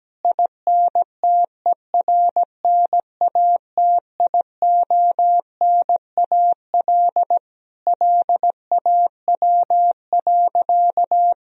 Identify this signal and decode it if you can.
Morse code, heard as a keyed tone, about 17 wpm, 700 Hz. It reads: INTERNATIONAL LAW.
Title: Sentences from Top 1000 Words - Review Part 4 - 17wpm